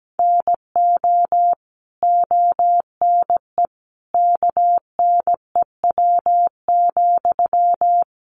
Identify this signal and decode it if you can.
Morse code, heard as a keyed tone, about 17 wpm, 700 Hz. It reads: NO ONE KNEW,
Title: Sentences from Top 1400 Words 17wpm